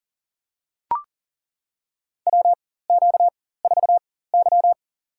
Morse code, heard as a keyed tone, about 40 wpm, spaced as for 19 wpm, 700 Hz. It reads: WQ4Y
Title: USA Call Signs - 4 Characters - 4x Character Spacing - 40wpm